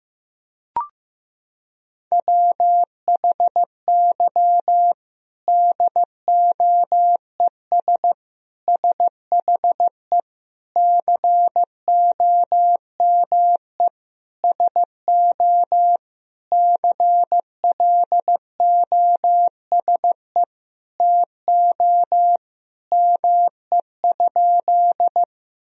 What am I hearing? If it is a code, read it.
Morse code, heard as a keyed tone, about 15 wpm, 700 Hz. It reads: WHY DOES SHE COME SO CLOSE TO ME?
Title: Sentences from Top 1000 Words - Review Part 1 - 15wpm